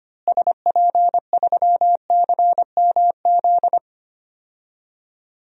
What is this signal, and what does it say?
Morse code, heard as a keyed tone, about 25 wpm, 700 Hz. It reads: SP3CMZ
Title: DXCC-100 Call Signs - 6 Characters - Repeated 2x - 25wpm